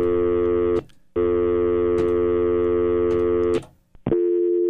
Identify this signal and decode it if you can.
Dial tone (North America), 350 + 440 Hz.